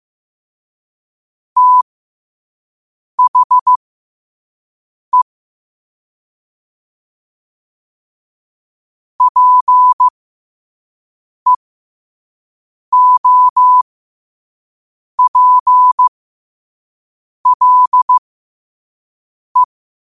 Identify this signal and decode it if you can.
Morse code, heard as a keyed tone, about 15 wpm, spaced as for 5 wpm, 1000 Hz. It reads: THE PEOPLE
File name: qotdfiles / 5wpm